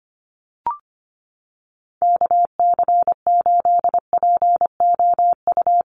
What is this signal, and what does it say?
Morse code, heard as a keyed tone, about 25 wpm, 700 Hz. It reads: KC8POU
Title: USA Call Signs Repeated 3x 25wpm